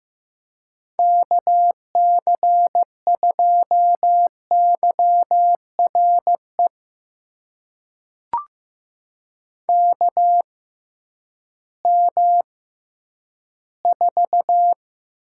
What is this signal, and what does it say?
Morse code, heard as a keyed tone, about 15 wpm, spaced as for 10 wpm, 700 Hz. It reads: KC2YRE KM4
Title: USA Call Signs - 6 Characters - 6x Character Spacing - 15wpm